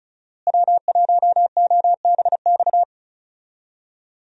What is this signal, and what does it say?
Morse code, heard as a keyed tone, about 36 wpm, 700 Hz. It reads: W1OBX
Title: USA Call Signs - 5 Characters - Repeated 3x - 35wpm